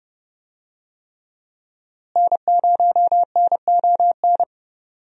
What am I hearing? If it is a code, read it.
Morse code, heard as a keyed tone, about 32 wpm, 700 Hz. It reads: N0NON
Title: USA Call Signs - 5 Characters - Repeated 3x - 30wpm